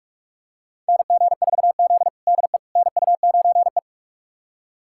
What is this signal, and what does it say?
Morse code, heard as a keyed tone, about 46 wpm, 700 Hz. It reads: NG4Z DE NU9E
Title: USA Callsign DE Callsign - 4 Characters - 45wpm